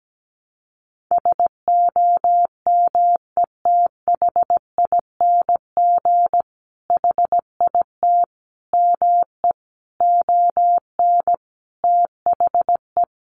Morse code, heard as a keyed tone, about 17 wpm, 700 Hz. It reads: SOMETHING HIT ME ON THE
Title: Sentences from Top 1200 Words 17wpm